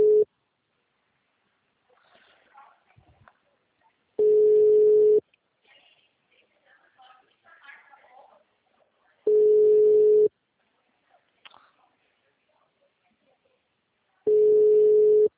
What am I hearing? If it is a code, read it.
Ringback tone (Europe, 425 Hz), with speech.
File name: out- -2016- - - .13570